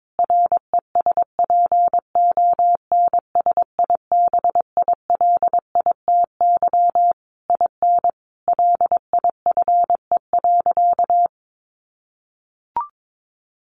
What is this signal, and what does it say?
Morse code, heard as a keyed tone, about 22 wpm, 700 Hz. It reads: RESPONSIBILITY IN LIFE.
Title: Sentences from Top 1200 Words 22wpm